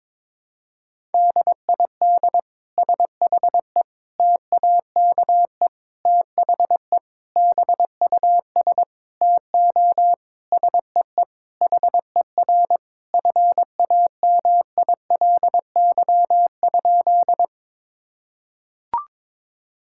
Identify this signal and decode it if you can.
Morse code, heard as a keyed tone, about 22 wpm, 700 Hz. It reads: DID SHE TAKE THE BUS TO SEE HER FAMILY?
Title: Sentences from Top 1000 Words - Review Part 6 - 22wpm